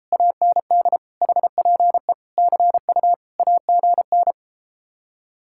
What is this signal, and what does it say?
Morse code, heard as a keyed tone, about 33 wpm, 700 Hz. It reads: AND HPE CU AGN